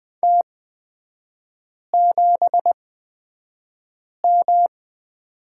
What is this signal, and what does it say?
Morse code, heard as a keyed tone, about 20 wpm, spaced as for 5 wpm, 700 Hz. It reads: T7M
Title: 243 Lesson 39e - Letter z - Three Characters 20wpm